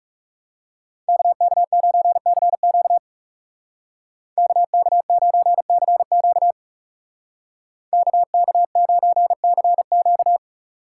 Morse code, heard as a keyed tone, about 38 wpm, 700 Hz. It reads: KK9CQ KK9CQ KK9CQ
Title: Speed Racing - USA 2x2 Call Signs 30wpm